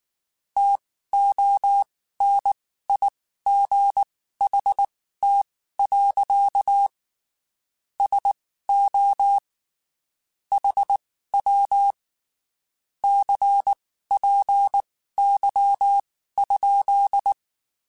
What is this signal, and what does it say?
Morse code, heard as a keyed tone, about 19 wpm, spaced as for 14 wpm, 800 Hz. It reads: TONIGHT. SO HW CPY?
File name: (spaced) / QSOs Spaced-mp3s / WA7QOX 1 @18 sp